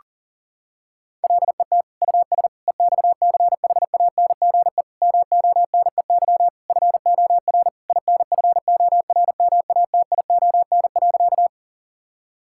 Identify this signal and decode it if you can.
Morse code, heard as a keyed tone, about 40 wpm, 700 Hz. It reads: LET US EXCHANGE MONEY FOR INFORMATION.
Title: Sentences from Top 1000 Words - Review Part 6 - 40wpm